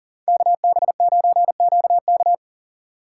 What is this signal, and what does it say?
Morse code, heard as a keyed tone, about 49 wpm, 700 Hz. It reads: KB9QK
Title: Speed Racing - USA Call Signs 40wpm